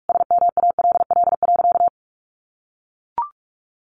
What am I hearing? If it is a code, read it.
Morse code, heard as a keyed tone, about 44 wpm, 700 Hz. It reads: SMALL.
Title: Sentences from Top 1500 Words 45wpm